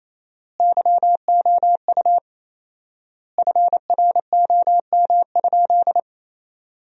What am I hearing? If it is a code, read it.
Morse code, heard as a keyed tone, about 28 wpm, 700 Hz. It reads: YOU FROM?
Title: 4x Word-Spacing - Sentences from Top 200 Words - 28wpm